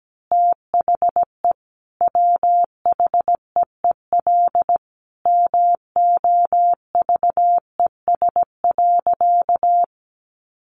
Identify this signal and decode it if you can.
Morse code, heard as a keyed tone, about 17 wpm, 700 Hz. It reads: THE WHEEL MOVES.
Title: Additional Easy Phrases 17wpm